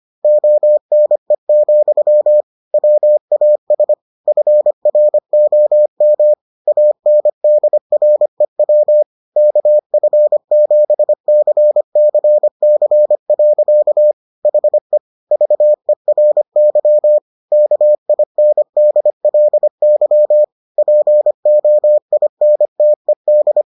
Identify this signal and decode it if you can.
Morse code, heard as a keyed tone, about 25 wpm, 600 Hz. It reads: ONE, WAS FROM ANDREW KF7CCC. HE VERY KINDLY POINTED